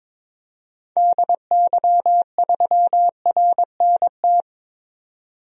Morse code, heard as a keyed tone, about 22 wpm, 700 Hz. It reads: DY3RNT
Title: DXCC-100 Call Signs - 6 Characters - 22wpm